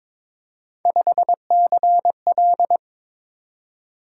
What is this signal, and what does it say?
Morse code, heard as a keyed tone, about 22 wpm, 700 Hz. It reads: 5CL
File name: practice / 22wpm / Three Letter-Number Combinations - 22wpm